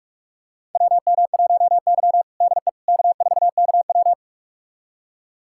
Morse code, heard as a keyed tone, about 46 wpm, 700 Hz. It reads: WM1Y DE K4KW
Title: USA Callsign DE Callsign - 4 Characters - 45wpm